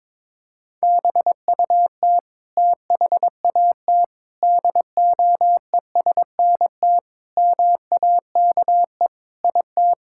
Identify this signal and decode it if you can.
Morse code, heard as a keyed tone, about 21 wpm, 700 Hz. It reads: BUT THAT DOESNT MAKE IT
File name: practice / 22wpm / Sets of 7 Words - 22wpm